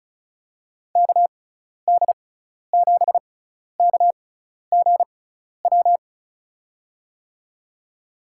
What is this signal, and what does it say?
Morse code, heard as a keyed tone, about 35 wpm, spaced as for 12 wpm, 700 Hz. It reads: KD7KGW
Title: USA Call Signs - 6 Characters - 6x Character Spacing - 35wpm